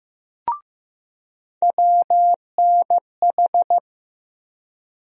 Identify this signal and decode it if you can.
Morse code, heard as a keyed tone, about 15 wpm, 700 Hz. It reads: WNH
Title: USA Call Sign Postfixes - 3 Letters 15wpm